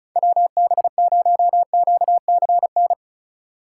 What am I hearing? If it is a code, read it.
Morse code, heard as a keyed tone, about 36 wpm, 700 Hz. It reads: WB0QCN
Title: USA Call Signs - 6 Characters - 3x Character Spacing - 35wpm